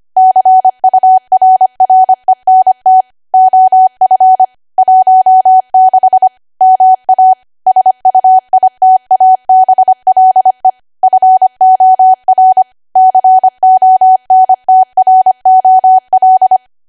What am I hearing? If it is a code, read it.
Morse code, heard as a keyed tone, about 25 wpm, 750 Hz. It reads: CURRENT OF 16 MA SUITABLE FOR CONTROL